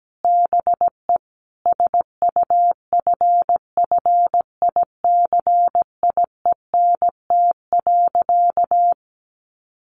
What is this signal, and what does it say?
Morse code, heard as a keyed tone, about 17 wpm, 700 Hz. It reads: BE SUFFICIENT.
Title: Sentences from Top 1400 Words 17wpm